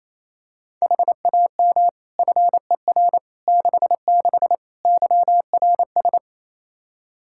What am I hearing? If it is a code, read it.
Morse code, heard as a keyed tone, about 27 wpm, 700 Hz. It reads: HAM FER 66 YRS
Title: Practice QSO - Exchange 3 - More Info - 28wpm